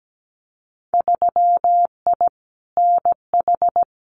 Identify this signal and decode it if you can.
Morse code, heard as a keyed tone, about 17 wpm, 700 Hz. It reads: 3I NH
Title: Winter Field Day Exchanges Encore 17wpm